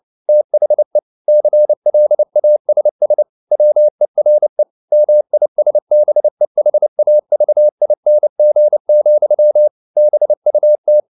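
Morse code, heard as a keyed tone, about 29 wpm, 600 Hz. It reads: THE CLASS WERE MISBEHAVING, BUT
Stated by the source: Morse code at 29 wpm, sending Ken Follett's novel Never in English, from CW med Gnister.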